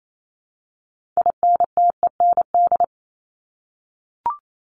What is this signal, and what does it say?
Morse code, heard as a keyed tone, about 28 wpm, 700 Hz. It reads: INTEND
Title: Top 1501-1750 Words 28wpm